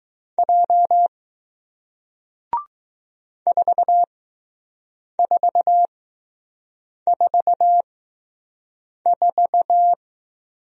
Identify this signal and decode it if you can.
Morse code, heard as a keyed tone, about 20 wpm, spaced as for 6 wpm, 700 Hz. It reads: J 4444